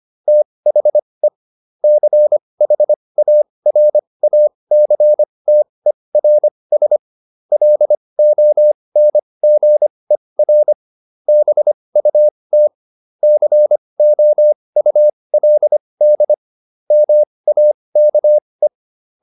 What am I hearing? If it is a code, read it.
Morse code, heard as a keyed tone, about 25 wpm, spaced as for 20 wpm, 600 Hz. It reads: THE CHARACTERS LONGER BUT COULD MAKE
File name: SS316 20